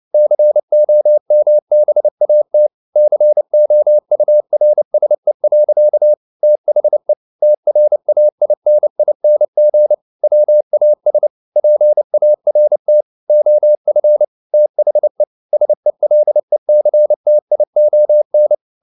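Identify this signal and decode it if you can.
Morse code, heard as a keyed tone, about 29 wpm, 600 Hz. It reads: COMBAT COURSE. THE TRAINING WAS PART OF THE SELECTION
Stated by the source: Morse code at 29 wpm, sending Ken Follett's novel Never in English, from CW med Gnister.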